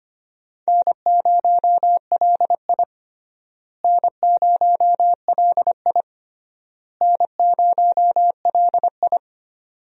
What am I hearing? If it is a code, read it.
Morse code, heard as a keyed tone, about 25 wpm, 700 Hz. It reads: N0LI N0LI N0LI
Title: USA Call Signs - 4 Characters - Repeated 3x - 25wpm